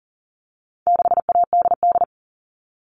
Morse code, heard as a keyed tone, about 40 wpm, 700 Hz. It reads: 6ADD
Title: All Cut Numbers - 4-Digit 40wpm